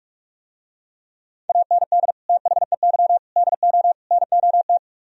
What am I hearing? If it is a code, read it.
Morse code, heard as a keyed tone, about 45 wpm, 700 Hz. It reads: AND THEY DO NOT